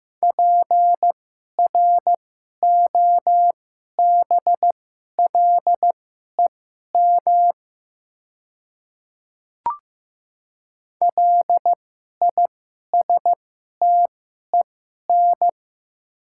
Morse code, heard as a keyed tone, about 15 wpm, spaced as for 11 wpm, 700 Hz. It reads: PROBLEM LISTEN
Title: 2x Character Spacing - Top 500 Words - 15wpm